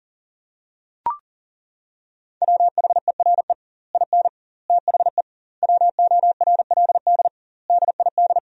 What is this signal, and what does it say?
Morse code, heard as a keyed tone, about 40 wpm, 700 Hz. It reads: WHERE IN THE WORLD DID
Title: 2x Word-Spacing - Sentences from Top 200 Words - 40wpm